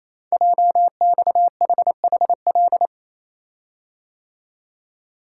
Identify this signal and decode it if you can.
Morse code, heard as a keyed tone, about 27 wpm, 700 Hz. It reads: JXHHL
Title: Code Groups - 5 Letters - 28wpm